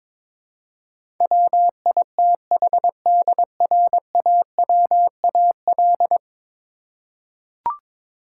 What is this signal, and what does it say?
Morse code, heard as a keyed tone, about 22 wpm, 700 Hz. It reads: WITHDRAWAL
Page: morse-code-ninja-practice-22wpm